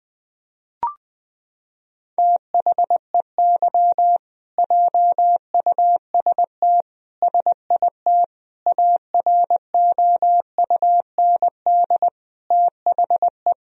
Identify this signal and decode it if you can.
Morse code, heard as a keyed tone, about 20 wpm, 700 Hz. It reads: THEY JUST SIT AROUND THE
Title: Sentences from Top 1000 Words - Review Part 3 - 20wpm